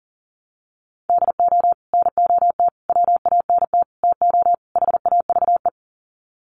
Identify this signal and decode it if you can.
Morse code, heard as a keyed tone, about 40 wpm, 700 Hz. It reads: DO NOT WANT TO HAVE
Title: Sets of 5 Words 40wpm